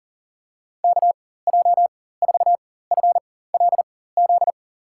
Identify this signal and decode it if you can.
Morse code, heard as a keyed tone, about 40 wpm, spaced as for 19 wpm, 700 Hz. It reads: KJ4FLZ
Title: USA Call Signs - 6 Characters - 4x Character Spacing - 40wpm